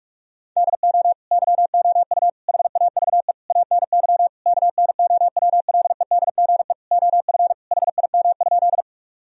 Morse code, heard as a keyed tone, about 41 wpm, 700 Hz. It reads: DO YOU HAVE ANY KNOWLEDGE OF HIM?
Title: Sentences from Top 1000 Words - Review Part 3 - 45wpm